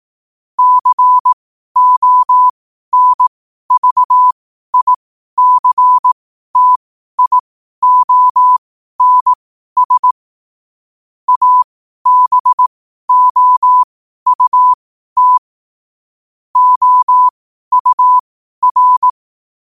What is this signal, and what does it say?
Morse code, heard as a keyed tone, about 18 wpm, spaced as for 12 wpm, 1000 Hz. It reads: CONVICTIONS ABOUT OUR
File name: qotdfiles / 12wpm